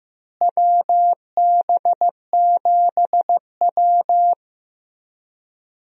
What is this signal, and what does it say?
Morse code, heard as a keyed tone, about 15 wpm, 700 Hz. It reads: WB7W
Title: USA Call Signs - 4 Characters - Repeated 3x - 15wpm